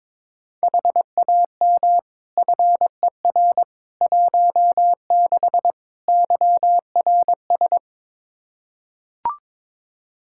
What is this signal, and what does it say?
Morse code, heard as a keyed tone, about 22 wpm, 700 Hz. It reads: HAM FER 16 YRS
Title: Practice QSO - Exchange 3 - More Info - 22wpm